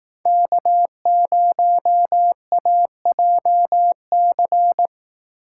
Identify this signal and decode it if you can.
Morse code, heard as a keyed tone, about 18 wpm, 700 Hz. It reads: K0AJC
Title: Speed Racing - USA 1x3 Call Signs 15wpm